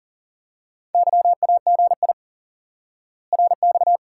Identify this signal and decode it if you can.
Morse code, heard as a keyed tone, about 40 wpm, 700 Hz. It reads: YAGI RX
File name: IQE Course 3 - Lesson 89 - YAGI - Two Elements - 40wpm